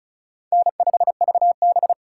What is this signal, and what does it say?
Morse code, heard as a keyed tone, about 36 wpm, 700 Hz. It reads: N5VB
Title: USA Call Signs - 4 Characters - 3x Character Spacing - 35wpm